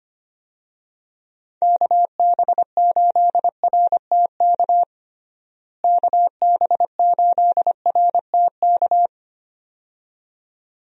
Morse code, heard as a keyed tone, about 25 wpm, 700 Hz. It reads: KB8RTK KB8RTK
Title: USA Call Signs - 6 Characters - Repeated 2x - 25wpm